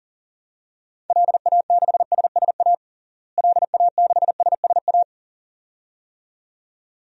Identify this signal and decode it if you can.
Morse code, heard as a keyed tone, about 40 wpm, 700 Hz. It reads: LA6SSA LA6SSA
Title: DXCC-100 Call Signs - 6 Characters - Repeated 2x - 40wpm